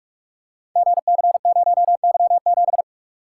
Morse code, heard as a keyed tone, about 42 wpm, 700 Hz. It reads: GC0Y7
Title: Code Groups - 5 Letter:Numbers - 45wpm